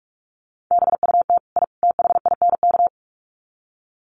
Morse code, heard as a keyed tone, about 44 wpm, 700 Hz. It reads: BUT I THINK
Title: Sets of 3 Words 45wpm